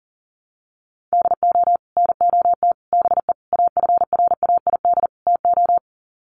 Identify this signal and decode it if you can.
Morse code, heard as a keyed tone, about 40 wpm, 700 Hz. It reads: DO NOT BE AFRAID TO
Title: Sets of 5 Words Encore2 40wpm